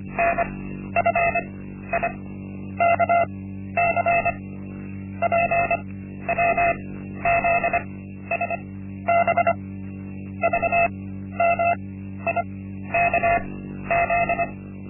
Morse code, heard as a keyed tone, about 25 wpm, spaced as for 12 wpm, 690 Hz. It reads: NFIKC PWZSB VMIKZ
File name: Linksmakalnis-1